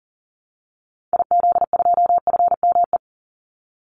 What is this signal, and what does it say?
Morse code, heard as a keyed tone, about 40 wpm, 700 Hz. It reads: IZ2FME
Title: Call Signs Repeated 2x - 40wpm